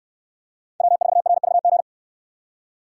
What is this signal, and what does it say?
Morse code, heard as a keyed tone, about 68 wpm, 700 Hz. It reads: W4RVG